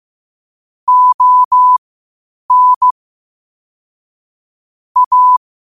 Morse code, heard as a keyed tone, about 15 wpm, spaced as for 8 wpm, 1000 Hz. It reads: ON A
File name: qotdfiles / 8wpm